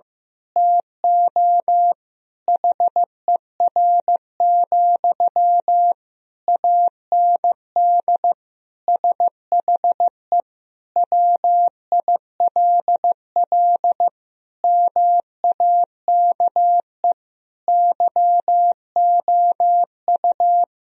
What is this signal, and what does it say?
Morse code, heard as a keyed tone, about 16 wpm, 700 Hz. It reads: TO HER, AND SHE WILL MAKE YOU
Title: Sentences from Top 1000 Words - Review Part 3 - 15wpm